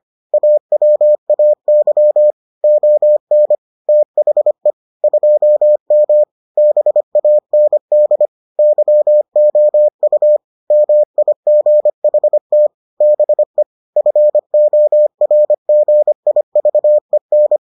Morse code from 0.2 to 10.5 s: AWAY ON THE 2M BAND YOU

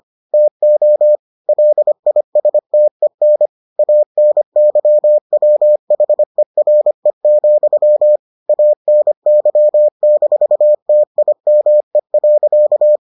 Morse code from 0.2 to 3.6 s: TO LISTEN

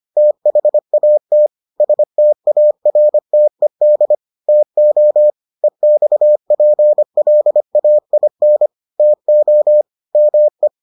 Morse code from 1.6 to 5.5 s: STARTED TO

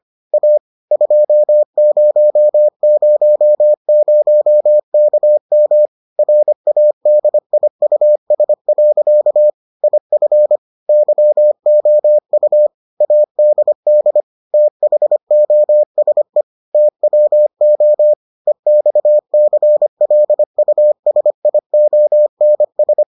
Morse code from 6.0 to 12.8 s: RADIUS. IF YOU

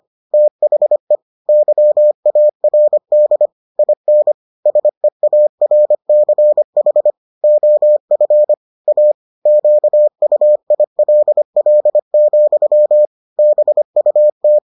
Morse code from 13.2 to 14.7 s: BUT

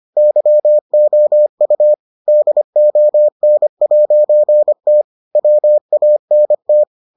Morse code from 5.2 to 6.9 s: WANT

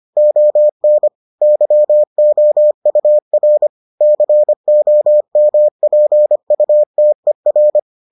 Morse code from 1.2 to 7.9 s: YOUR COMPUTER

